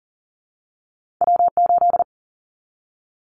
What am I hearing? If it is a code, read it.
Morse code, heard as a keyed tone, about 40 wpm, 700 Hz. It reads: W8
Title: Callsign Prefixes for Top 25 DXCC - Repeated 3x - 40wpm